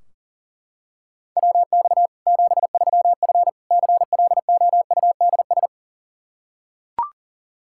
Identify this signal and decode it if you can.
Morse code, heard as a keyed tone, about 40 wpm, 700 Hz. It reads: WX 73F CLOUDS
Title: Practice QSO - Exchange 2 - Info - 40wpm